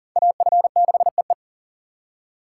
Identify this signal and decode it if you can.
Morse code, heard as a keyed tone, about 50 wpm, 700 Hz. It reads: AF6EE